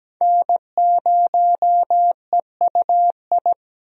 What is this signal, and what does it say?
Morse code, heard as a keyed tone, about 17 wpm, 700 Hz. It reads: N0EUI
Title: USA Call Signs - 5 Characters - 17wpm